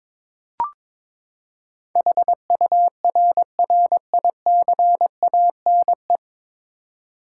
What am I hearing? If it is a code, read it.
Morse code, heard as a keyed tone, about 22 wpm, 700 Hz. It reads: HURRICANE
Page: Morse Code Ninja practice 22wpm